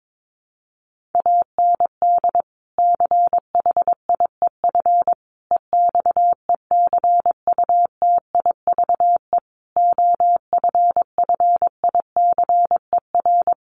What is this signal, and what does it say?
Morse code, heard as a keyed tone, about 22 wpm, 700 Hz. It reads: AND CHIEF EXECUTIVE OFFICER
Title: Sets of 5 Words Encore2 22wpm